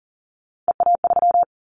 Morse code, heard as a keyed tone, about 40 wpm, 700 Hz. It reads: EA3
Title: Callsign Prefixes for Top 25 DXCC - 40wpm